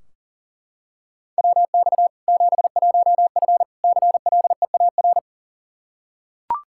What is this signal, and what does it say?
Morse code, heard as a keyed tone, about 40 wpm, 700 Hz. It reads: WX 71F CLEAR
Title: K1USN SST QSO Practice 40wpm